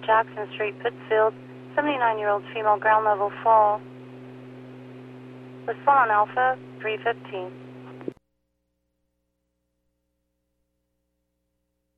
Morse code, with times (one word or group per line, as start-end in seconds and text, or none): none